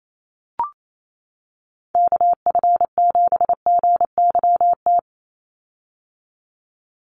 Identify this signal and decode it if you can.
Morse code, heard as a keyed tone, about 28 wpm, 700 Hz. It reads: KF7GYT
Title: USA Call Signs 28wpm